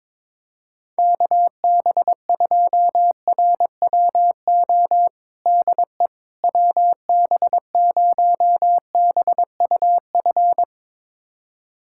Morse code, heard as a keyed tone, about 22 wpm, 700 Hz. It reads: KB2RWO DE WB0BUF
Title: USA Callsign DE Callsign - 6 Characters - 22wpm